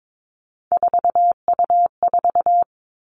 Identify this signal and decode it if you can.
Morse code, heard as a keyed tone, about 22 wpm, 700 Hz. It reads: 4U4
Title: Three Challenging Letter-Number Combinations 22wpm